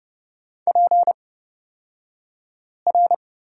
Morse code, heard as a keyed tone, about 30 wpm, spaced as for 5 wpm, 700 Hz. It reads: PR
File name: Lesson 37d - Letter x - Two Characters 30wpm